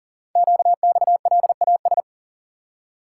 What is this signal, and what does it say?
Morse code, heard as a keyed tone, about 39 wpm, 700 Hz. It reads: QXLAS
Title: Code Groups - 5 Letters - 40wpm